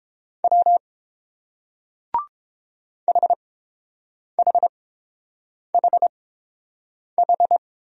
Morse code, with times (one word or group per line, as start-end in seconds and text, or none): 0.4-0.8 W
3.1-7.6 HHHH